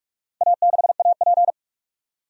Morse code, heard as a keyed tone, about 42 wpm, 700 Hz. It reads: A6AP
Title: DXCC-100 Call Signs - 4 Characters - 45wpm